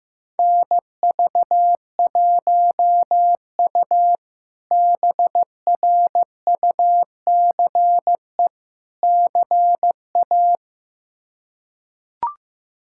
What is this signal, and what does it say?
Morse code, heard as a keyed tone, about 15 wpm, 700 Hz. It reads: NV1U BRUCE CA
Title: K1USN SST QSO Practice - 15wpm